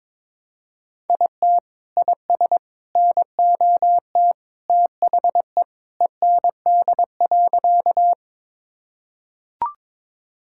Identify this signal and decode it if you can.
Morse code, heard as a keyed tone, about 21 wpm, 700 Hz. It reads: IT IS NOT THE END.
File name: practice / 22wpm / Sentences with 3-Word Phrases - 22wpm